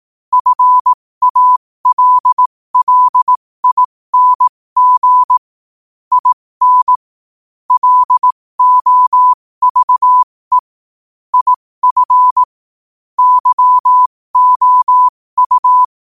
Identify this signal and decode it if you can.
Morse code, heard as a keyed tone, about 18 wpm, spaced as for 15 wpm, 1000 Hz. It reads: FALLING IN LOVE IF YOU